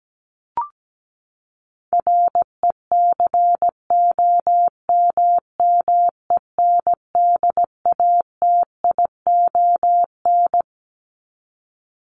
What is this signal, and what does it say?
Morse code, heard as a keyed tone, about 17 wpm, 700 Hz. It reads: RECOMMENDATION
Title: Top 1251-1500 Words 17wpm